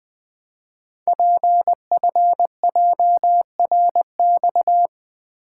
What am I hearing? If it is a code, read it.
Morse code, heard as a keyed tone, about 20 wpm, 700 Hz. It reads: PFJRX
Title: Code Groups - 5 Letters - 20wpm